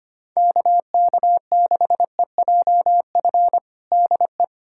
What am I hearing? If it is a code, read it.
Morse code, heard as a keyed tone, about 25 wpm, 700 Hz. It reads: KK6EJF DE
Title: USA Callsign DE Callsign 25wpm